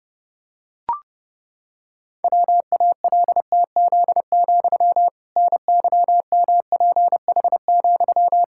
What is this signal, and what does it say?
Morse code, heard as a keyed tone, about 30 wpm, 700 Hz. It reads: WALTZ, NYMPH,
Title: Pangram Challenge v2 - 30wpm